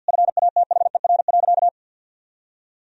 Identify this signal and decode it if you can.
Morse code, heard as a keyed tone, about 50 wpm, 700 Hz. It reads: FATHER.
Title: Sentences from Top 1000 Words - Review Part 4 - 50wpm